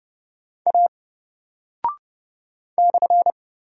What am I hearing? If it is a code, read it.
Morse code, heard as a keyed tone, about 28 wpm, 700 Hz. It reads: A /
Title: Single Character (Letter:Number:Punctuation:Prosign) 30wpm